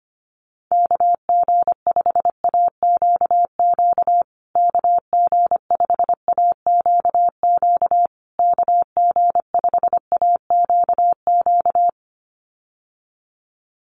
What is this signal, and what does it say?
Morse code, heard as a keyed tone, about 25 wpm, 700 Hz. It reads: KG5AQQ KG5AQQ KG5AQQ
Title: USA Call Signs Repeated 3x 25wpm